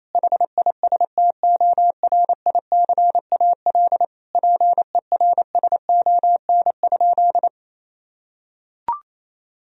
Morse code, heard as a keyed tone, about 28 wpm, 700 Hz. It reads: HISTORICAL PERSON?
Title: Sentences from Top 1000 Words - Review Part 4 - 28wpm